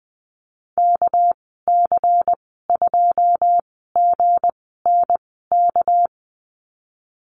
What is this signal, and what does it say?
Morse code, heard as a keyed tone, about 20 wpm, spaced as for 15 wpm, 700 Hz. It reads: KC2GNK
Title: 207 Lesson 33g - Letter j - Call Signs 20wpm